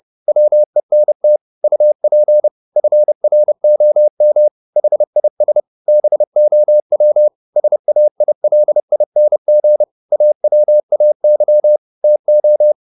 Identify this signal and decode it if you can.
Morse code, heard as a keyed tone, about 30 wpm, 600 Hz. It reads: WENT UP FROM HIS BOW SAILING AWAY TO